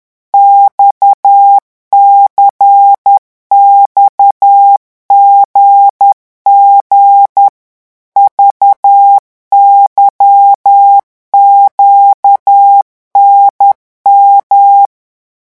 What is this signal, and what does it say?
Morse code, heard as a keyed tone, about 10 wpm, 800 Hz. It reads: XCXGG VYQNM
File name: morse mp3 / revision3-vitesse 10 mots